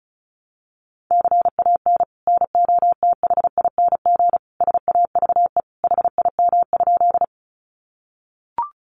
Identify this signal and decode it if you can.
Morse code, heard as a keyed tone, about 34 wpm, 700 Hz. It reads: CAN NOTHING SAVE HIM?
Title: Sentences from Top 1100 Words 35wpm